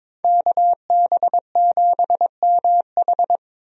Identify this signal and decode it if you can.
Morse code, heard as a keyed tone, about 22 wpm, 700 Hz. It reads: KB7MH
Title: USA Call Signs - 5 Characters - Repeated 3x - 22wpm